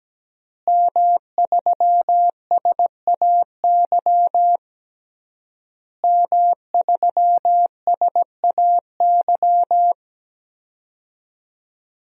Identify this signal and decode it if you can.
Morse code, heard as a keyed tone, about 17 wpm, 700 Hz. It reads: M3SAY M3SAY
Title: DXCC-100 Call Signs - 5 Characters - Repeated 3x - 17wpm